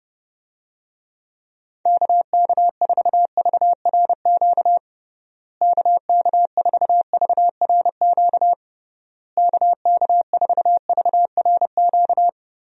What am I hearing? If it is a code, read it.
Morse code, heard as a keyed tone, about 30 wpm, 700 Hz. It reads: KK4VRQ KK4VRQ KK4VRQ
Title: USA Call Signs - 6 Characters - Repeated 3x - 30wpm